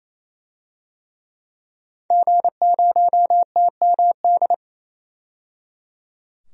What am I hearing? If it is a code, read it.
Morse code, heard as a keyed tone, about 28 wpm, 700 Hz. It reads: G0TMD
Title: DXCC-100 Call Signs - 5 Characters - 28wpm